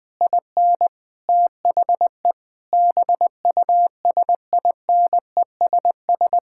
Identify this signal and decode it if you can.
Morse code, heard as a keyed tone, about 20 wpm, 700 Hz. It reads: IN THE BUSINESS
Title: Sentences from Top 1000 Words - Review Part 5 - 20wpm